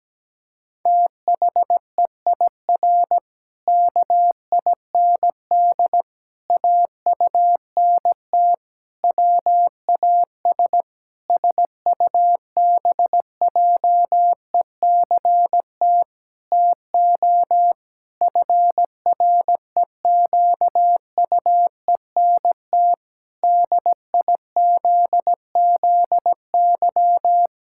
Morse code, heard as a keyed tone, about 17 wpm, 700 Hz. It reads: THEIR KIND AUNT WAS SUBJECT TO FREQUENT DIZZY
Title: Pangram Challenge v2 - 17wpm